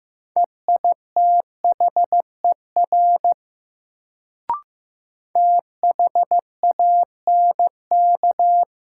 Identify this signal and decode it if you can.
Morse code, heard as a keyed tone, about 15 wpm, 700 Hz. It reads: EITHER THANK
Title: Sentences from Top 600 Words - New Word Drill - 15wpm